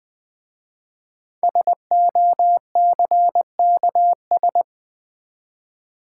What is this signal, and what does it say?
Morse code, heard as a keyed tone, about 20 wpm, 700 Hz. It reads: SOCKS
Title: Easy Words with Hints - 20wpm